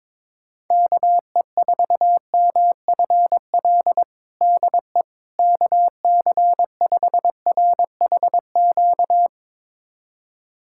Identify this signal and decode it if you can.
Morse code, heard as a keyed tone, about 22 wpm, 700 Hz. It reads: KE4MFL DE KC5RHQ
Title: USA Callsign DE Callsign 22wpm